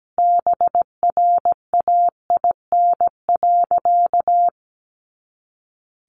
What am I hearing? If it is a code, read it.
Morse code, heard as a keyed tone, about 17 wpm, 700 Hz. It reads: BRAIN.
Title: A Sound Of Thunder 17wpm